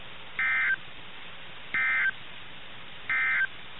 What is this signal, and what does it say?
Emergency Alert System (SAME) tones, heard as harsh buzzing bursts. It is heard from 0.3 to 3.5 s.
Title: NOAA Weather Radio - KIH61